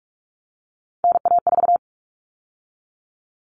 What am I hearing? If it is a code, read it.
Morse code, heard as a keyed tone, about 44 wpm, 700 Hz. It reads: NA4
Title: All Cut Numbers - 3-Digit 45wpm